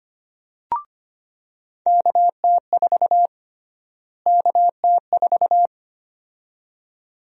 Morse code, heard as a keyed tone, about 25 wpm, 700 Hz. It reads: KT4 KT4
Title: USA Call Sign Prefixes - Repeated 2x - 25wpm